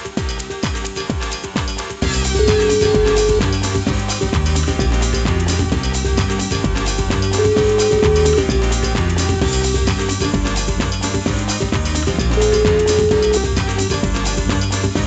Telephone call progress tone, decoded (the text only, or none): ringback tone